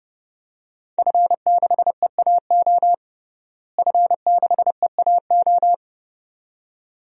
Morse code, heard as a keyed tone, about 30 wpm, 700 Hz. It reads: F6EAO F6EAO
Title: DXCC-100 Call Signs - 5 Characters - Repeated 2x - 30wpm